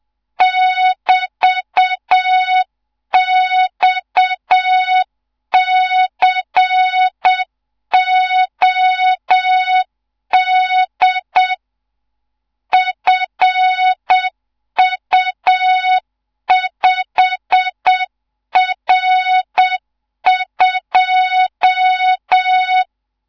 Morse code, heard as a keyed tone, about 7 wpm, 3700 Hz. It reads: <BT>XCOD FU5R2
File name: Morze-HA2MN / hang / 14 280 HA2MN